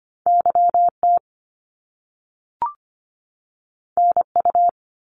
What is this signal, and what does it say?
Morse code, heard as a keyed tone, about 25 wpm, 700 Hz. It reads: YT NU
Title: Canadian Provinces and Territories 25wpm